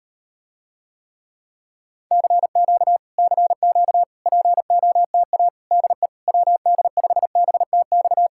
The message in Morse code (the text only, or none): CQ CQ POTA DE WD5BTX